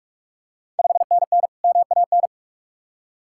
Morse code, heard as a keyed tone, about 46 wpm, 700 Hz. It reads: 5NN MAN
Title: Ontario QSO Party Exchange 45wpm